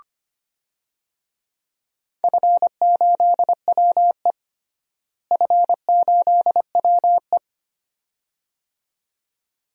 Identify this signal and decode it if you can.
Morse code, heard as a keyed tone, about 25 wpm, 700 Hz. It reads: F8WE F8WE
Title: DXCC-100 Call Signs - 4 Characters - Repeated 2x - 25wpm